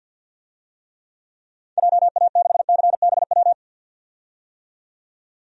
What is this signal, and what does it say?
Morse code, heard as a keyed tone, about 52 wpm, 700 Hz. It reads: JA6CBW
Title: DXCC-100 Call Signs - 6 Characters - 50wpm